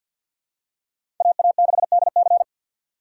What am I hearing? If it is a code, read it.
Morse code, heard as a keyed tone, about 53 wpm, 700 Hz. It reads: AA6DC
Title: USA Call Signs - 5 Characters - 4x Character Spacing - 50wpm